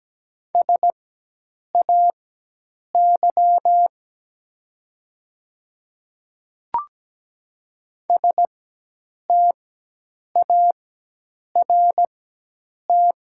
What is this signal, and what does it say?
Morse code, heard as a keyed tone, about 17 wpm, spaced as for 8 wpm, 700 Hz. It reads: SAY START
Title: 4x Character Spacing - Top 300 Words - 17wpm